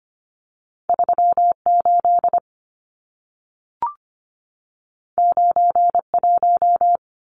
Morse code, heard as a keyed tone, about 25 wpm, 700 Hz. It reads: 38 91
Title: Numbers - 2-Digit 25wpm